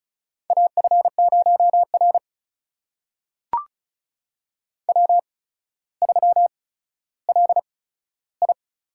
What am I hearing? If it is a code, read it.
Morse code, heard as a keyed tone, about 36 wpm, spaced as for 15 wpm, 700 Hz. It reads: AF0R W3LI
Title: USA Call Signs - 8x Character Spacing - 35wpm